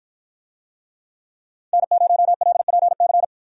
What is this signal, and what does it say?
Morse code, heard as a keyed tone, about 53 wpm, 700 Hz. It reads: N0LPC